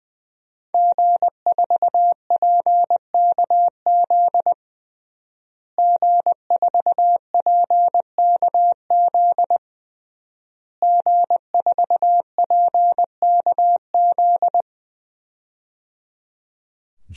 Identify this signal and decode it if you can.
Morse code, heard as a keyed tone, about 20 wpm, 700 Hz. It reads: G4PKZ G4PKZ G4PKZ
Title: DXCC-100 Call Signs - 5 Characters - Repeated 3x - 20wpm